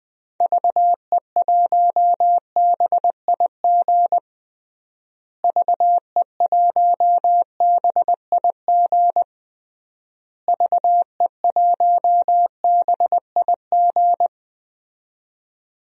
Morse code, heard as a keyed tone, about 21 wpm, 700 Hz. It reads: VE1BIG VE1BIG VE1BIG
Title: DXCC-100 Call Signs - 6 Characters - Repeated 3x - 20wpm